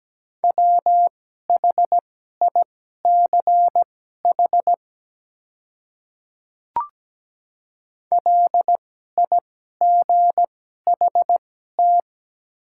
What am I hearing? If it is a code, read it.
Morse code, heard as a keyed tone, about 17 wpm, spaced as for 12 wpm, 700 Hz. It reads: WHICH LIGHT
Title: 2x Character Spacing - Top 200 Words - 17wpm